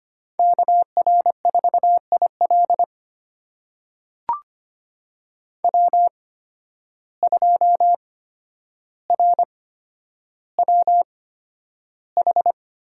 Morse code, heard as a keyed tone, about 25 wpm, spaced as for 11 wpm, 700 Hz. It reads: KR4IL W2RWH
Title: USA Call Signs - 5 Characters - 8x Character Spacing - 25wpm